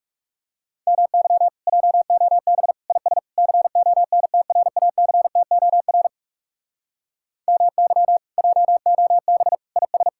Morse code, heard as a keyed tone, about 40 wpm, 700 Hz. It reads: MY JOB IS CONTRACTOR MY JOB IS